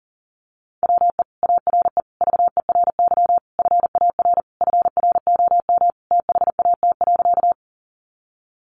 Morse code, heard as a keyed tone, about 40 wpm, 700 Hz. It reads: WE ARE VERY FAR FROM THAT.
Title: Sentences from Top 300 Words 40wpm